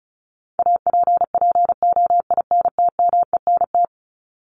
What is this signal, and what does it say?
Morse code, heard as a keyed tone, about 35 wpm, 700 Hz. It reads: APPOINTMENT
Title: Top 1751-2000 Words 35wpm